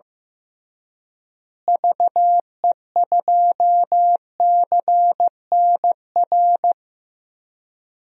Morse code, heard as a keyed tone, about 15 wpm, 700 Hz. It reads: VE2CNR
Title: DXCC-100 Call Signs - 6 Characters - Repeated 3x - 15wpm